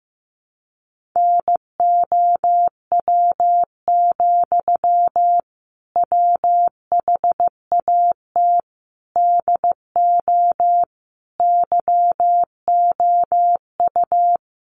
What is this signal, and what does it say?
Morse code, heard as a keyed tone, about 15 wpm, 700 Hz. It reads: NOW, WHAT DO YOU
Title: Sentences from Top 200 Words 15wpm